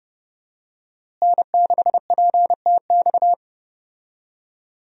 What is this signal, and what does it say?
Morse code, heard as a keyed tone, about 30 wpm, 700 Hz. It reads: N6PTX
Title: USA Call Signs - 5 Characters - Repeated 3x - 30wpm